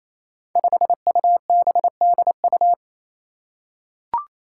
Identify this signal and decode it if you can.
Morse code, heard as a keyed tone, about 28 wpm, 700 Hz. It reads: 5UBDU
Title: Five Challenging Letter-Number Combinations 28wpm